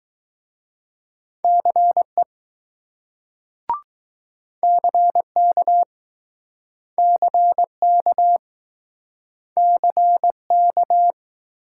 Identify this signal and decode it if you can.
Morse code, heard as a keyed tone, about 20 wpm, 700 Hz. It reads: CE CK CK CK